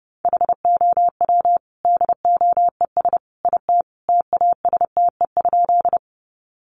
Morse code, heard as a keyed tone, about 30 wpm, 700 Hz. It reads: HOW DOES IT TASTE?
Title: Sentences from Top 1100 Words 30wpm